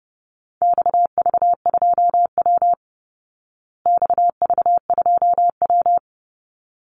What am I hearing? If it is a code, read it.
Morse code, heard as a keyed tone, about 30 wpm, 700 Hz. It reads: XV2W XV2W
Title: Call Signs Repeated 3x - 30wpm